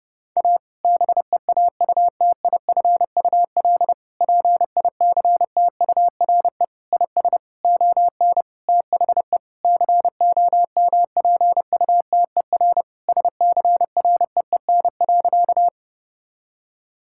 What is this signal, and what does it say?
Morse code, heard as a keyed tone, about 30 wpm, 700 Hz. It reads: A BEAUTIFUL PICTURE IS ON THE COMPUTER SCREEN.
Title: Sentences from Top 1000 Words - Review Part 6 - 30wpm